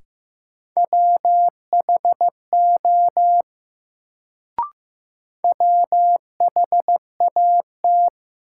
Morse code, heard as a keyed tone, about 15 wpm, 700 Hz. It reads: WHO WHAT
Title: Sets of 2 Words - New Word Drill - 15wpm